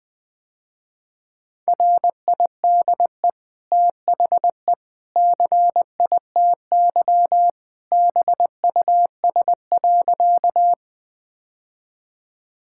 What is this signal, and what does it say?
Morse code, heard as a keyed tone, about 20 wpm, 700 Hz. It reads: RIDE THE CITY BUS.
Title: Easy Sentences with Hints - 20wpm